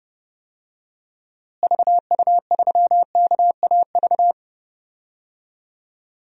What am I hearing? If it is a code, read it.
Morse code, heard as a keyed tone, about 30 wpm, 700 Hz. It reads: VU3KAV
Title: DXCC-100 Call Signs - 6 Characters - Repeated 3x - 30wpm